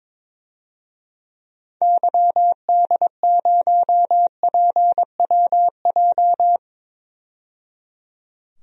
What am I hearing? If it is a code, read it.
Morse code, heard as a keyed tone, about 21 wpm, 700 Hz. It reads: YD0PWJ
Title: DXCC-100 Call Signs - 6 Characters - 22wpm